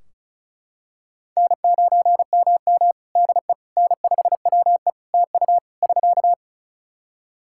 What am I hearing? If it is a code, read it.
Morse code, heard as a keyed tone, about 36 wpm, 700 Hz. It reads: N9MM DE N5WE TU <SK>
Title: Practice QSO - Exchange 4 - Ending - 35wpm